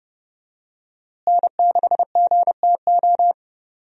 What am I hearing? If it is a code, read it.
Morse code, heard as a keyed tone, about 32 wpm, 700 Hz. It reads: N6GTO
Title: USA Call Signs - 5 Characters - Repeated 3x - 30wpm